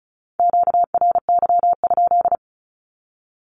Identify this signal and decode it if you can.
Morse code, heard as a keyed tone, about 36 wpm, 700 Hz. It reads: QRY?
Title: Q-Codes with Spoken Meaning - 35wpm